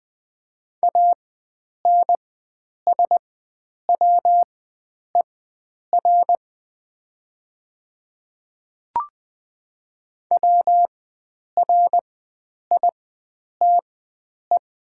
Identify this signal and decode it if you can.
Morse code, heard as a keyed tone, about 20 wpm, spaced as for 9 wpm, 700 Hz. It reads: ANSWER WRITE
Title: 4x Character Spacing - Top 300 Words - 20wpm